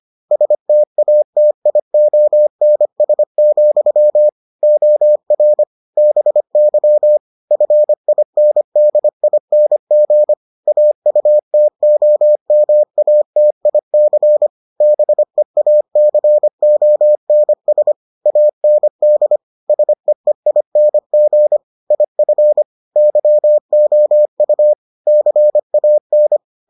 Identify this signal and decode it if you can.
Morse code, heard as a keyed tone, about 25 wpm, 600 Hz. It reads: STATIONS, OR BY FINDING AUTOMATIC BEACONS AND SEEING IF YOU CAN